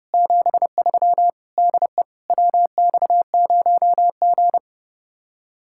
Morse code, heard as a keyed tone, about 32 wpm, 700 Hz. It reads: 73 DE WX0G